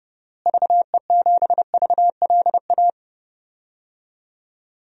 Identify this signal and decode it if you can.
Morse code, heard as a keyed tone, about 29 wpm, 700 Hz. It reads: VE7VLA
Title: DXCC-100 Call Signs - 6 Characters - Repeated 2x - 30wpm